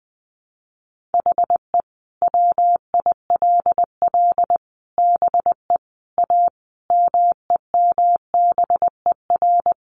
Morse code, heard as a keyed tone, about 20 wpm, 700 Hz. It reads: HE WILL BE A MEMBER
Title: Sentences from Top 1100 Words 20wpm